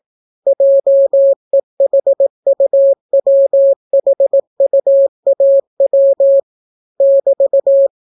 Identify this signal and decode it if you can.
Morse code, heard as a keyed tone, about 18 wpm, 550 Hz. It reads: JEHUWHUAW <BT>